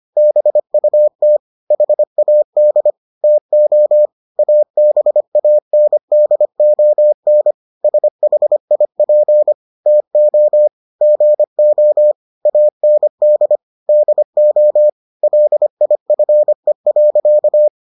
Morse code, heard as a keyed tone, about 25 wpm, 600 Hz. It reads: BUT HAD TO ABANDON SHIP TO GO AND DO LIFE.